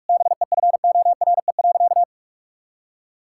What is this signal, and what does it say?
Morse code, heard as a keyed tone, about 46 wpm, 700 Hz. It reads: BEFORE.
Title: Sentences from Top 1000 Words - Review Part 5 - 45wpm